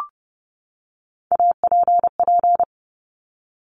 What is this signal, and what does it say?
Morse code, heard as a keyed tone, about 30 wpm, 690 Hz. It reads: APP
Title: Top Three Letter Combinations in Words 30wpm